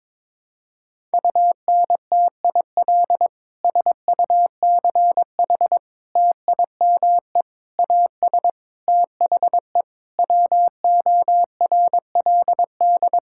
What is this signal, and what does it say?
Morse code, heard as a keyed tone, about 22 wpm, 700 Hz. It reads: UNTIL SUCH TIME AS THE WORLD
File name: Sets of 9 Words - 22wpm